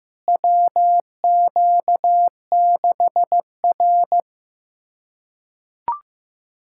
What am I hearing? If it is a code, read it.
Morse code, heard as a keyed tone, about 15 wpm, 700 Hz. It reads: WQ6R
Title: USA Callsign DE Callsign - 4 Characters - 15wpm